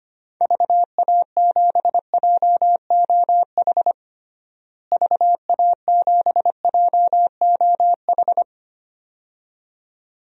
Morse code, heard as a keyed tone, about 25 wpm, 700 Hz. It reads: VA7JOH VA7JOH
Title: DXCC-100 Call Signs - 6 Characters - Repeated 2x - 25wpm